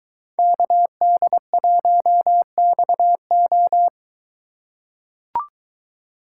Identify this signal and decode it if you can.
Morse code, heard as a keyed tone, about 23 wpm, 700 Hz. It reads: KD1XO